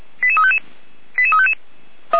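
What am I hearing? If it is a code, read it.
Five-tone selective call (ZVEI), heard as quick runs of five tones, tones of about 80 ms. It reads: 88355 88355
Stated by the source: ZVEI five-tone selective call 88247, received by SDR, from audio.f85360.feuerwehr-orlamuende.org fire brigade archive.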